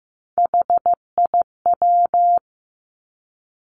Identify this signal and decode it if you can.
Morse code, heard as a keyed tone, about 15 wpm, 700 Hz. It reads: HIW
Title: Three Challenging Letter-Number Combinations 15wpm